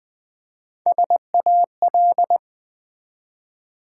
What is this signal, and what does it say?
Morse code, heard as a keyed tone, about 20 wpm, 700 Hz. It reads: SAL
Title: Build A Word - 1 Letter At A Time - 4-Letter Words - 20wpm